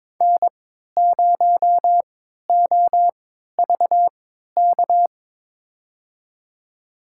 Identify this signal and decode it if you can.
Morse code, heard as a keyed tone, about 22 wpm, spaced as for 13 wpm, 700 Hz. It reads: N0OVK
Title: USA Call Signs - 5 Characters - 3x Character Spacing - 22wpm